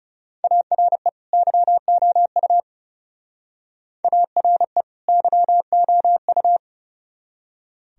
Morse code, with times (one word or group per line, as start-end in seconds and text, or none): 0.4-1.1 ARE
1.3-2.6 YOU
4.0-4.8 ARE
5.1-6.6 YOU